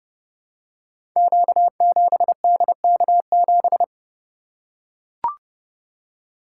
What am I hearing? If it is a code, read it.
Morse code, heard as a keyed tone, about 28 wpm, 700 Hz. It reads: Q7DK7
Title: Code Groups - 5 Letter:Numbers - 30wpm